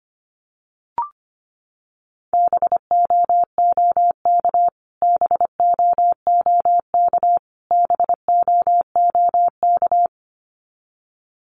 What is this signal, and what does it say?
Morse code, heard as a keyed tone, about 25 wpm, 700 Hz. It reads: BOOK BOOK BOOK
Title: Top 300 Words Repeated 25wpm